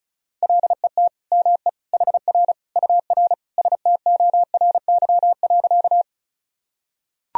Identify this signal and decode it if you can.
Morse code, heard as a keyed tone, about 34 wpm, 700 Hz. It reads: LET ME HR UR STORY.
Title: Sentences from Top 300 Words and Abbreviations 35wpm